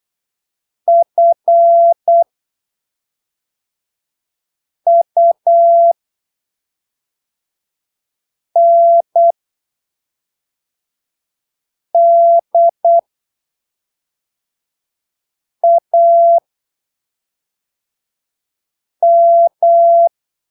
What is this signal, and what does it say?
Morse code, heard as a keyed tone, about 8 wpm, spaced as for 3 wpm, 680 Hz. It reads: FUNDAM
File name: arquivos / texto / texto 09 08ppm